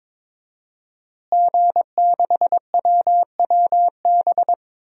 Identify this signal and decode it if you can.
Morse code, heard as a keyed tone, about 22 wpm, 700 Hz. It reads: G6WWB
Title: DXCC-100 Call Signs - 5 Characters - Repeated 2x - 22wpm